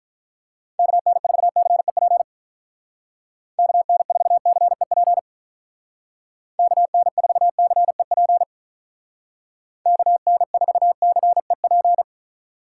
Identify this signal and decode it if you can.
Morse code, heard as a keyed tone, about 46 wpm, 700 Hz. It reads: KN4CEP KN4CEP KN4CEP KN4CEP